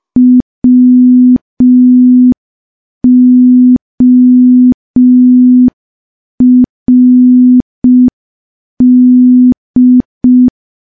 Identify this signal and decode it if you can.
Morse code, heard as a keyed tone, about 5 wpm, 250 Hz. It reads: WORD
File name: morse / morse-training-05WPM